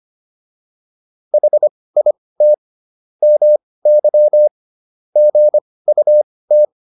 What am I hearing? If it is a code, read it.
Morse code, heard as a keyed tone, about 25 wpm, spaced as for 18 wpm, 600 Hz. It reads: HIT MY GUT